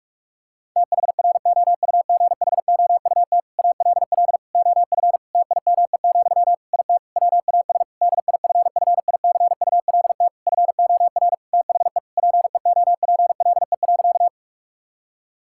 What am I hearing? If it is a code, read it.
Morse code, heard as a keyed tone, about 41 wpm, 700 Hz. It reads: THROUGHOUT ALL OF TIME, IT WAS DIFFICULT FOR THE PEOPLE.
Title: Sentences from Top 1000 Words - Review Part 5 - 45wpm